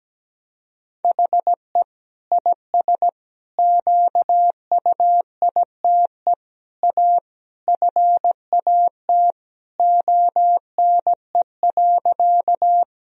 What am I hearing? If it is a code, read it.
Morse code, heard as a keyed tone, about 17 wpm, 700 Hz. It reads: HE IS QUITE A FAT ONE.
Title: Sentences from Top 1000 Words - Review Part 4 - 17wpm